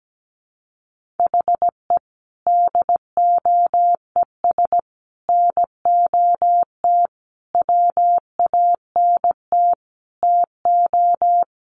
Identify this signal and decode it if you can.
Morse code, heard as a keyed tone, about 17 wpm, 700 Hz. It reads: HE DOES NOT WANT TO
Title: Sets of 5 Words 17wpm